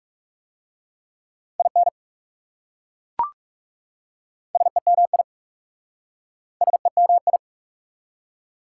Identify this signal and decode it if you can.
Morse code, heard as a keyed tone, about 40 wpm, 690 Hz. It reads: IN SEMI SEMI